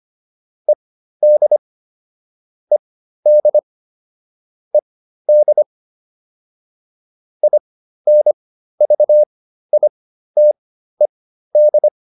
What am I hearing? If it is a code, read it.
Morse code, heard as a keyed tone, about 25 wpm, spaced as for 13 wpm, 600 Hz. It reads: ED ED ED INVITED